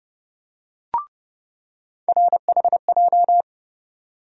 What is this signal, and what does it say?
Morse code, heard as a keyed tone, about 30 wpm, 700 Hz. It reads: RHJ
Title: USA Call Sign Postfixes - 3 Letters 30wpm